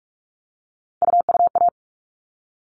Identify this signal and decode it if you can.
Morse code, heard as a keyed tone, about 44 wpm, 700 Hz. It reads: UUA
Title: All Cut Numbers - 3-Digit 45wpm